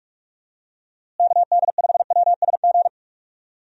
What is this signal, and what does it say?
Morse code, heard as a keyed tone, about 41 wpm, 700 Hz. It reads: KD5WSG
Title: USA Call Signs - 6 Characters - Repeated 3x - 45wpm